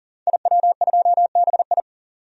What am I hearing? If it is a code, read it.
Morse code, heard as a keyed tone, about 40 wpm, 700 Hz. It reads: IW2BI
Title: DXCC-100 Call Signs - 5 Characters - 40wpm